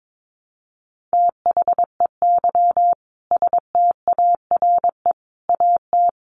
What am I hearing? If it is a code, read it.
Morse code, heard as a keyed tone, about 22 wpm, 700 Hz. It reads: THEY STARE AT
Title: Easy Phrases 22wpm